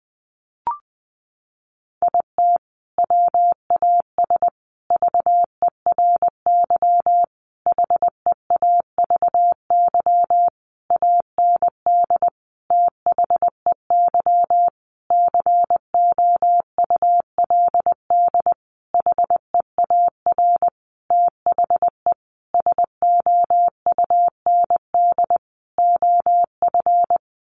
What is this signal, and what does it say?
Morse code, heard as a keyed tone, about 20 wpm, 700 Hz. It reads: IT WAS VERY HEAVY AND THEY COULD HEAR THE SOUND OF
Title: Sentences from Top 1100 Words 20wpm